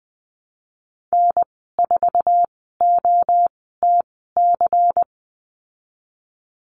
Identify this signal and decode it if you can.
Morse code, heard as a keyed tone, about 20 wpm, spaced as for 15 wpm, 700 Hz. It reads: N4OTC
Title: 154 Lesson 25g - Letter p - Call Signs 20wpm